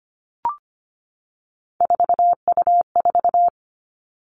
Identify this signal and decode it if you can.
Morse code, heard as a keyed tone, about 25 wpm, 700 Hz. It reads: 4U4
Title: Three Challenging Letter-Number Combinations 25wpm